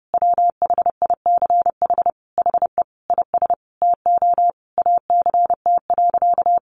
Morse code, heard as a keyed tone, about 30 wpm, 700 Hz. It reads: WHICH HE IS TO ACT.
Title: Sentences from Top 1200 Words 30wpm